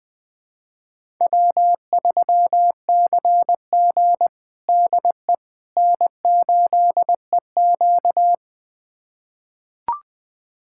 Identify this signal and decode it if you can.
Morse code, heard as a keyed tone, about 21 wpm, 700 Hz. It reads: W3CG DE N8EQ
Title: USA Callsign DE Callsign - 4 Characters - 20wpm